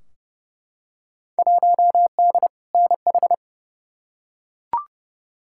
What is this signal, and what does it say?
Morse code, heard as a keyed tone, about 29 wpm, 700 Hz. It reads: 1D NH
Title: Field Day Exchanges Encore - 30wpm